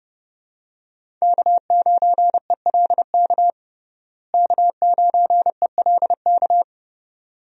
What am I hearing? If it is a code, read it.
Morse code, heard as a keyed tone, about 30 wpm, 700 Hz. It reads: K9ELK K9ELK
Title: USA Call Signs - 5 Characters - Repeated 2x - 30wpm